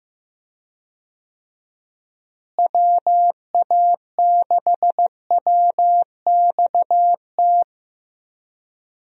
Morse code, heard as a keyed tone, about 15 wpm, 700 Hz. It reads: WA6WXT
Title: USA Call Signs - 6 Characters - Repeated 2x - 15wpm